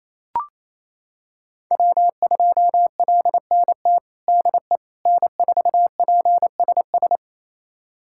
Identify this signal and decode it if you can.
Morse code, heard as a keyed tone, about 28 wpm, 700 Hz. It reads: W2LNT DE N4PSS
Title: USA Callsign DE Callsign - 5 Characters - 28wpm